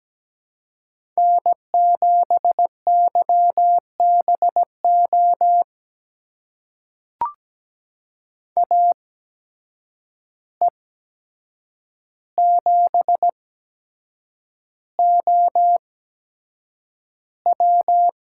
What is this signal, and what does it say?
Morse code, heard as a keyed tone, about 17 wpm, spaced as for 7 wpm, 700 Hz. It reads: N7YBO AE7OW
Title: USA Call Signs - 5 Characters - 8x Character Spacing - 17wpm